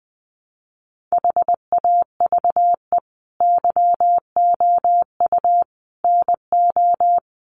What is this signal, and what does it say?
Morse code, heard as a keyed tone, about 20 wpm, 700 Hz. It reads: HAVE YOU NO
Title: Sentences from Top 1500 Words 20wpm